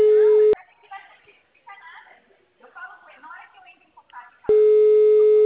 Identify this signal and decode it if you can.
Ringback tone (Europe, 425 Hz), with speech.
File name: out- -2003- - - .176